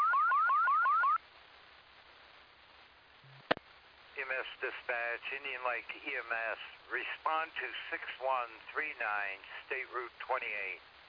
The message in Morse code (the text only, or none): none